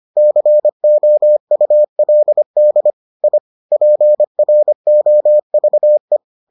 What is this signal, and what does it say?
Morse code, heard as a keyed tone, about 25 wpm, 600 Hz. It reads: COULD I PROVE